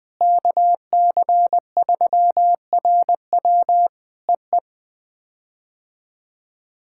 Morse code, heard as a keyed tone, about 20 wpm, 700 Hz. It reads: KC3RW EE
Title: POTA QSO Practice - 20wpm